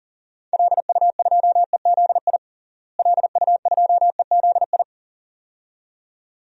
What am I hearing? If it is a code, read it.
Morse code, heard as a keyed tone, about 37 wpm, 700 Hz. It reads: LU2EZI LU2EZI
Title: DXCC-100 Call Signs - 6 Characters - Repeated 2x - 40wpm